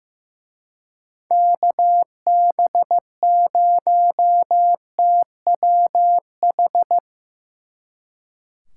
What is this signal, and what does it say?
Morse code, heard as a keyed tone, about 17 wpm, 700 Hz. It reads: KB0TWH
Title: Speed Racing - USA Call Signs 15wpm